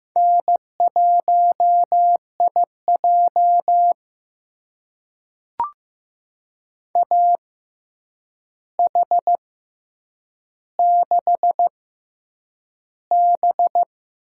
Morse code, heard as a keyed tone, about 15 wpm, spaced as for 8 wpm, 700 Hz. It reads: N1IJ AH6B
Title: USA Call Signs - 4 Characters - 6x Character Spacing - 15wpm